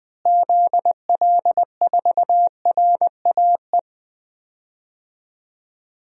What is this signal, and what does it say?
Morse code, heard as a keyed tone, about 19 wpm, 700 Hz. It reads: ZL4RAE
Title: DXCC-100 Call Signs - 6 Characters - Repeated 3x - 20wpm